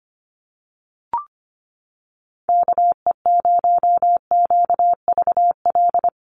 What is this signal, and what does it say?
Morse code, heard as a keyed tone, about 25 wpm, 700 Hz. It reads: KE0QVL
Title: USA Call Signs Repeated 2x 25wpm